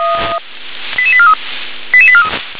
Five-tone selective call (ZVEI), heard as quick runs of five tones, tones of about 70 ms. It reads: 88843 88843
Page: ZVEI